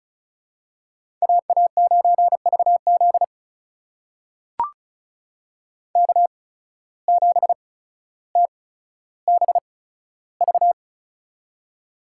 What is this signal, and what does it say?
Morse code, heard as a keyed tone, about 36 wpm, spaced as for 15 wpm, 700 Hz. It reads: AA9VZ K7TBV
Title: USA Call Signs - 5 Characters - 8x Character Spacing - 35wpm